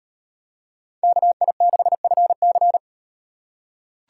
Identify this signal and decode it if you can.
Morse code, heard as a keyed tone, about 35 wpm, 700 Hz. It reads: KI6FC